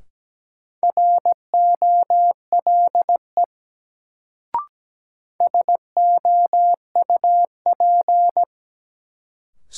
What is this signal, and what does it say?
Morse code, heard as a keyed tone, about 17 wpm, 700 Hz. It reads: ROLE SOUP
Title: Sentences from Top 1000 Words - Review Part 6 - New Word Drill - 17wpm